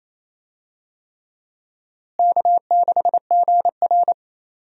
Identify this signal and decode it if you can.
Morse code, heard as a keyed tone, about 28 wpm, 700 Hz. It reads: K6GR
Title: USA Call Signs - 4 Characters - Repeated 3x - 28wpm